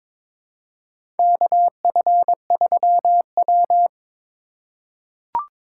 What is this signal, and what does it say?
Morse code, heard as a keyed tone, about 22 wpm, 700 Hz. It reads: KF3W
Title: USA Call Signs - 4 Characters - 3x Character Spacing - 22wpm